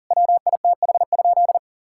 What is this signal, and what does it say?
Morse code, heard as a keyed tone, about 40 wpm, 700 Hz. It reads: WITH?
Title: Sentences from Top 1000 Words - Review Part 5 - 40wpm